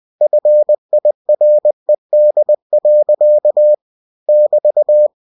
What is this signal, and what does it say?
Morse code, heard as a keyed tone, about 20 wpm, 600 Hz. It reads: FIRED. <BT>